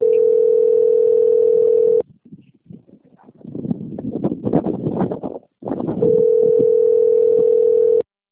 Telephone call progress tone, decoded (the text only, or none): ringback tone